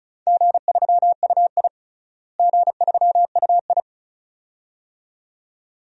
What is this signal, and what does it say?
Morse code, heard as a keyed tone, about 36 wpm, 700 Hz. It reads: G3UI G3UI
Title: DXCC-100 Call Signs - 4 Characters - Repeated 2x - 35wpm